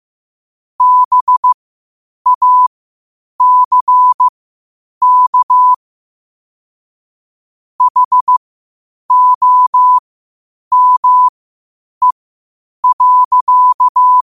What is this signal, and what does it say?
Morse code, heard as a keyed tone, about 15 wpm, spaced as for 8 wpm, 1000 Hz. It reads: BACK HOME.